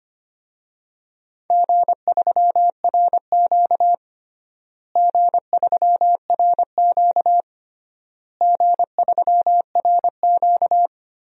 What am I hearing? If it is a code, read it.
Morse code, heard as a keyed tone, about 25 wpm, 700 Hz. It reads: G3RQ G3RQ G3RQ
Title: DXCC-100 Call Signs - 4 Characters - Repeated 3x - 25wpm